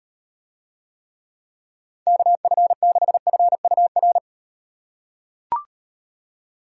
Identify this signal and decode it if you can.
Morse code, heard as a keyed tone, about 38 wpm, 700 Hz. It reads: KF6FUR